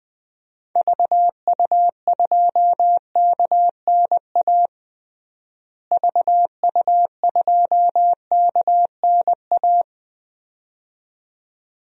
Morse code, heard as a keyed tone, about 21 wpm, 700 Hz. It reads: VU2KNA VU2KNA
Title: DXCC-100 Call Signs - 6 Characters - Repeated 2x - 20wpm